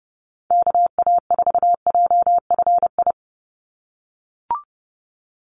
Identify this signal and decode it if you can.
Morse code, heard as a keyed tone, about 30 wpm, 700 Hz. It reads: KA4JFI
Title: USA Call Signs 30wpm